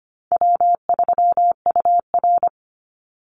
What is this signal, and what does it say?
Morse code, heard as a keyed tone, about 25 wpm, 700 Hz. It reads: W3UR
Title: Easy Call Signs Repeated 3x - 25wpm